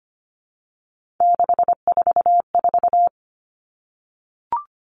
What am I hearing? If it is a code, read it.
Morse code, heard as a keyed tone, about 25 wpm, 700 Hz. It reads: 644
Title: Numbers - 3-Digit 25wpm